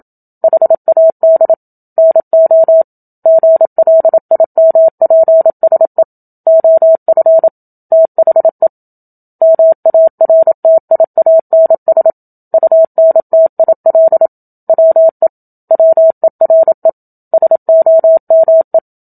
Morse code, heard as a keyed tone, about 27 wpm, 650 Hz. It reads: HAD NO GLIMPSE OF THE MARTIANS UNTIL WE WERE SOME
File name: Book Two / 01-Chapter 1 - 27 WPM